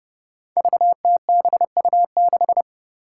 Morse code, heard as a keyed tone, about 29 wpm, 700 Hz. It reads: VTBU6
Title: Five Challenging Letter-Number Combinations 30wpm